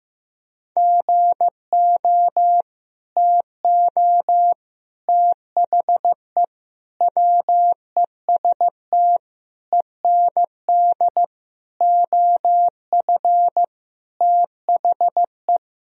Morse code, heard as a keyed tone, about 15 wpm, 700 Hz. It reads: GO TO THE WEST END OF THE
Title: Sentences from Top 1000 Words - Review Part 2 - 15wpm